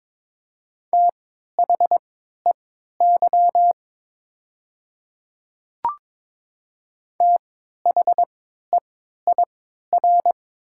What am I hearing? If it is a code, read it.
Morse code, heard as a keyed tone, about 23 wpm, spaced as for 13 wpm, 700 Hz. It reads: THEY THEIR
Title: 3x Character Spacing - Top 100 Words - 22wpm